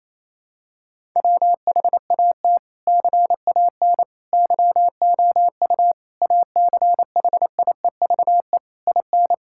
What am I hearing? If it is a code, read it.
Morse code, heard as a keyed tone, about 28 wpm, 700 Hz. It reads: WHAT CAN YOU ACHIEVE IN
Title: Sentences from Top 1000 Words - Review Part 6 - 28wpm